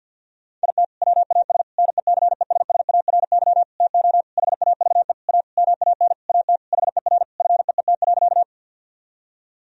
Morse code, heard as a keyed tone, about 51 wpm, 700 Hz. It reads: IT WAS NECESSARY TO HAVE A MAN AT HER FEET.
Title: Sentences from Top 1000 Words - Review Part 1 - 50wpm